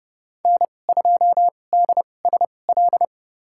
Morse code, heard as a keyed tone, about 30 wpm, spaced as for 22 wpm, 700 Hz. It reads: N2DSL
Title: USA Call Signs - 2x Character Spacing - 30wpm